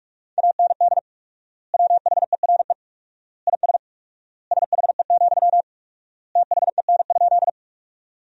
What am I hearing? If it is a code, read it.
Morse code, heard as a keyed tone, about 46 wpm, 700 Hz. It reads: AND WHERE IS SHE, THEN?
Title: 4x Word-Spacing - Sentences from Top 200 Words - 45wpm